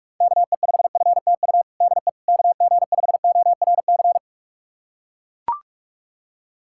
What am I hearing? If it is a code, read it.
Morse code, heard as a keyed tone, about 43 wpm, 690 Hz. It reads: KE5FTU DE KG5ORC
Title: USA Callsign DE Callsign 45wpm